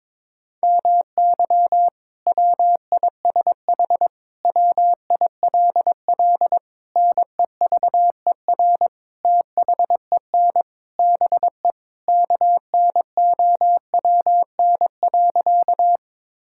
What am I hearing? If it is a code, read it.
Morse code, heard as a keyed tone, about 22 wpm, 700 Hz. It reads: MY WISH WILL NEVER THEN BE KNOWN.
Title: Sentences from Top 1600 Words 22wpm